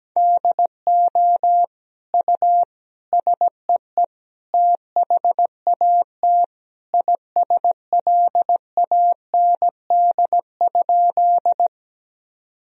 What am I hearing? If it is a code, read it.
Morse code, heard as a keyed tone, about 17 wpm, 700 Hz. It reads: DO U SEE THAT ISLAND?
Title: Sentences from Top 500 Words and Abbreviations 17wpm